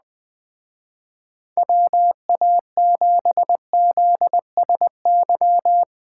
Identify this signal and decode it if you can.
Morse code, heard as a keyed tone, about 20 wpm, 700 Hz. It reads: WA7ZSY
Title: Speed Racing - USA 2x3 Call Signs 17wpm